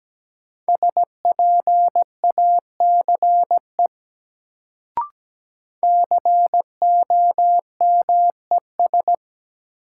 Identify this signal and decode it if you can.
Morse code, heard as a keyed tone, about 17 wpm, 700 Hz. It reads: SPACE COMES
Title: Sentences from Top 1000 Words - Review Part 5 - New Word Drill - 17wpm